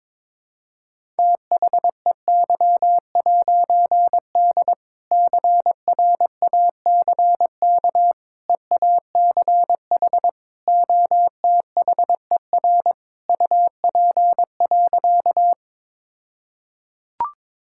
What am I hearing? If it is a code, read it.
Morse code, heard as a keyed tone, about 22 wpm, 700 Hz. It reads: THEY'D CRACK EACH OTHER UP.
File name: All-Age Appropriate Jokes - 22wpm